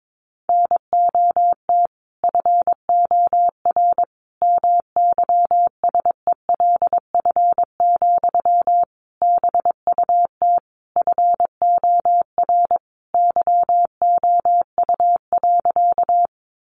Morse code, heard as a keyed tone, about 22 wpm, 700 Hz. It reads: NOT FOR MYSELF, BUT FOR YOU.
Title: Sentences from Top 1300 Words 22wpm